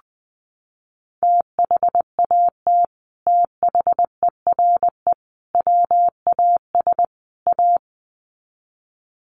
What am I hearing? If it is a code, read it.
Morse code, heard as a keyed tone, about 19 wpm, 700 Hz. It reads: THAT THERE WAS A
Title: Sets of 4 Words 20wpm